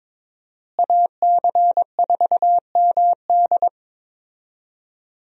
Morse code, heard as a keyed tone, about 21 wpm, 700 Hz. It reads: AC4MD
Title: USA Call Signs - 5 Characters - Repeated 3x - 22wpm